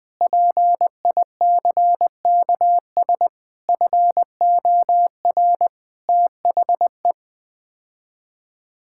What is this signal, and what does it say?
Morse code, heard as a keyed tone, about 20 wpm, 700 Hz. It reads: PICKS FOR THE
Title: Sets of 9 Words - 20wpm